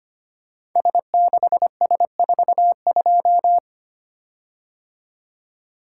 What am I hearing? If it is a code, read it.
Morse code, heard as a keyed tone, about 25 wpm, 700 Hz. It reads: S6S42
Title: Code Groups - 5 Letter:Numbers - 25wpm